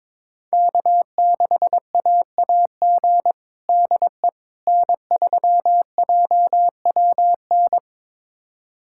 Morse code, heard as a keyed tone, about 22 wpm, 700 Hz. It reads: K6AAG DE N3JWN
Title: USA Callsign DE Callsign - 5 Characters - 22wpm